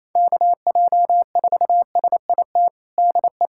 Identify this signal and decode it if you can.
Morse code, heard as a keyed tone, about 28 wpm, 700 Hz. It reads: KJ4SIT DE
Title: USA Callsign DE Callsign - 6 Characters - 28wpm